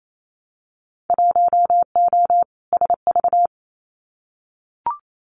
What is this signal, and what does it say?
Morse code, heard as a keyed tone, about 28 wpm, 700 Hz. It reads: 1O SV
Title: Winter Field Day Exchanges 28wpm